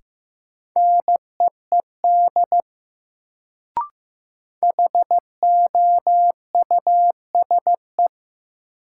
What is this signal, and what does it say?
Morse code, heard as a keyed tone, about 15 wpm, 700 Hz. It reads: NEED HOUSE
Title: Sets of 2 Words - New Word Drill - 15wpm